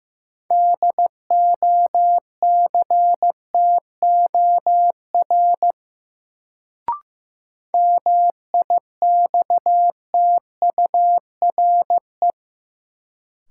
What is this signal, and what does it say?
Morse code, heard as a keyed tone, about 15 wpm, 700 Hz. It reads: DOCTOR MIXTURE
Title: Sentences from Top 1500 Words - New Word Drill - 15wpm